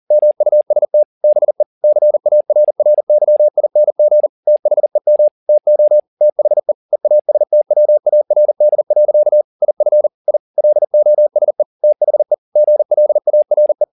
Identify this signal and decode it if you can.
Morse code, heard as a keyed tone, about 40 wpm, 600 Hz. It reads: MUST BE CARRYING THEM TO THE EASTWARD. IF I LOSE THE GLARE